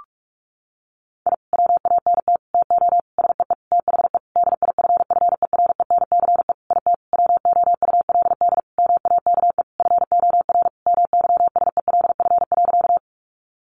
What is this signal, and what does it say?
Morse code, heard as a keyed tone, about 44 wpm, 700 Hz. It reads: I WANT TO SEE THE DIFFERENCE IT WOULD MAKE FOR MYSELF.
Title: Sentences from Top 1300 Words 45wpm